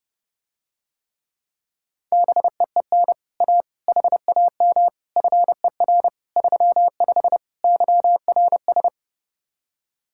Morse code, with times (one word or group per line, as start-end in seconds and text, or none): 2.1-3.1 BEEN
3.4-3.6 A
3.9-4.9 HAM
5.2-6.1 FER
6.4-7.4 35
7.6-8.9 YRS